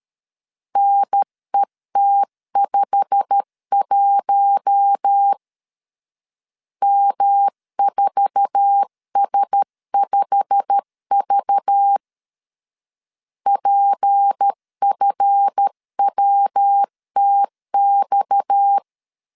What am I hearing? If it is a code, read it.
Morse code, heard as a keyed tone, about 13 wpm, 800 Hz. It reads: NET51 M4S5V PFWTX
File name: LEZIONI / 8 YJ45 / 8 YJ45 12wpm 2151